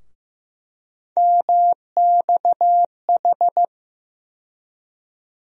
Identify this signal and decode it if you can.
Morse code, heard as a keyed tone, about 15 wpm, 700 Hz. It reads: MXH
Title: USA Call Sign Postfixes - 3 Letters 15wpm